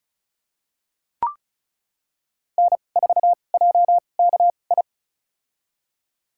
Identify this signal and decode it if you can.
Morse code, heard as a keyed tone, about 36 wpm, spaced as for 25 wpm, 700 Hz. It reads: N4JKI
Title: USA Call Signs - 5 Characters - 2x Character Spacing - 35wpm